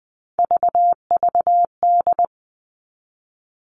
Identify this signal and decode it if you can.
Morse code, heard as a keyed tone, about 20 wpm, 700 Hz. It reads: VVD
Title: Three Challenging Letter-Number Combinations 20wpm